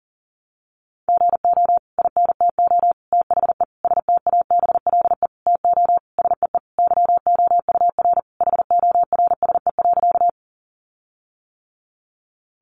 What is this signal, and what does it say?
Morse code, heard as a keyed tone, about 40 wpm, 700 Hz. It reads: GO INTO THE STABLE TO SEE YOUR HORSE.
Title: Sentences from Top 1500 Words 40wpm